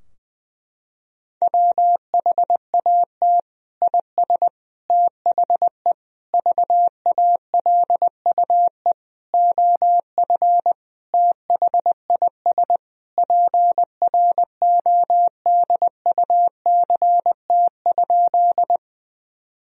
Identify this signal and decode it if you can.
Morse code, heard as a keyed tone, about 20 wpm, 700 Hz. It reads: WHAT IS THE VALUE OF THIS PRODUCT?
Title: Sentences from Top 1000 Words - Review Part 3 - 20wpm